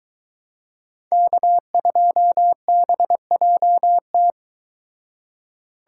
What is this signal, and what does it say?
Morse code, heard as a keyed tone, about 23 wpm, 700 Hz. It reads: K2BJT